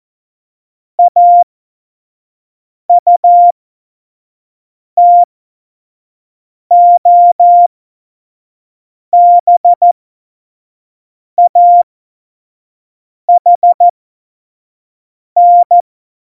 Morse code, heard as a keyed tone, about 14 wpm, spaced as for 5 wpm, 700 Hz. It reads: AUTOBAHN
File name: Morsekurs / Witze / Witz 19. Februar 2019, 25 F70